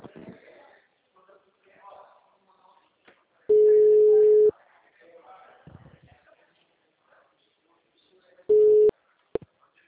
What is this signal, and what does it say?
Ringback tone (Europe, 425 Hz), with speech.
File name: out- -2007- - - .39778